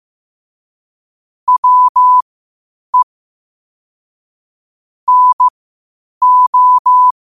Morse code, heard as a keyed tone, about 15 wpm, spaced as for 8 wpm, 1000 Hz. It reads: WE NO
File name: qotdfiles / 8wpm